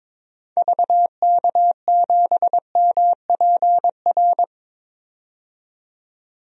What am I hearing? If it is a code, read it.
Morse code, heard as a keyed tone, about 22 wpm, 700 Hz. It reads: VK7MPR
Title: DXCC-100 Call Signs - 6 Characters - 22wpm